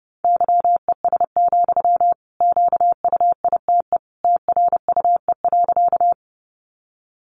Morse code, heard as a keyed tone, about 30 wpm, 700 Hz. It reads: YES, QUITE TRUE.
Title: Sentences from Top 700 Words 30wpm